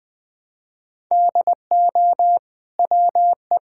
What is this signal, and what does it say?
Morse code, heard as a keyed tone, about 21 wpm, 700 Hz. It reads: DO WE
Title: Sentences from Top 1000 Words - Review Part 5 - 20wpm